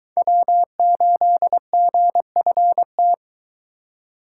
Morse code, heard as a keyed tone, about 22 wpm, 700 Hz. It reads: W8GFT